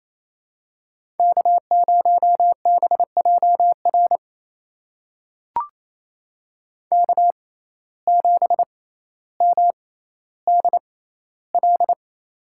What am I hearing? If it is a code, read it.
Morse code, heard as a keyed tone, about 28 wpm, spaced as for 14 wpm, 700 Hz. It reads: K0BJR K7MDL
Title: USA Call Signs - 5 Characters - 6x Character Spacing - 28wpm